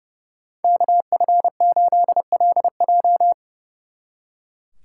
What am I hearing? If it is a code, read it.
Morse code, heard as a keyed tone, about 37 wpm, 700 Hz. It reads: KF8LJ
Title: Speed Racing - USA 2x2 Call Signs 30wpm